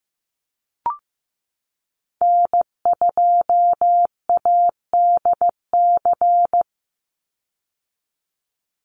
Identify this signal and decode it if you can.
Morse code, heard as a keyed tone, about 15 wpm, 700 Hz. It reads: N2ADC
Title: Call Signs Repeated 3x - 15wpm